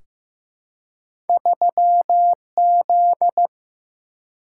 Morse code, heard as a keyed tone, about 15 wpm, 700 Hz. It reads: 3Z
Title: Single-Letter Single-Number 15wpm